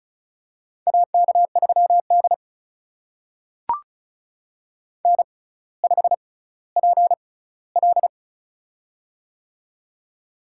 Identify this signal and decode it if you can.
Morse code, heard as a keyed tone, about 36 wpm, spaced as for 18 wpm, 700 Hz. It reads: AK3D N5PL
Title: USA Call Signs - 4 Characters - 6x Character Spacing - 35wpm